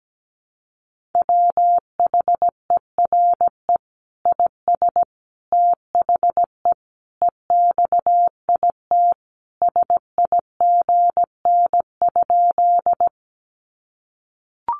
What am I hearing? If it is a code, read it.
Morse code, heard as a keyed tone, about 17 wpm, 700 Hz. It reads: WHERE IS THE EXIT SIGN?
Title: Sentences from Top 1100 Words 17wpm